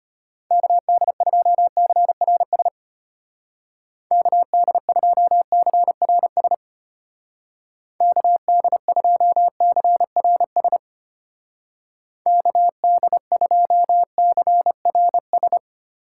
Morse code, heard as a keyed tone, about 33 wpm, 700 Hz. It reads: KD2CRS KD2CRS KD2CRS KD2CRS